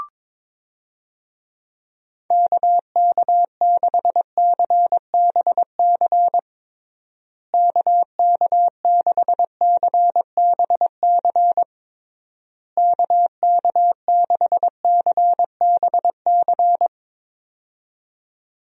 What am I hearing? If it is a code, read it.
Morse code, heard as a keyed tone, about 22 wpm, 700 Hz. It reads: KK6CBC KK6CBC KK6CBC
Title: USA Call Signs - 6 Characters - Repeated 3x - 22wpm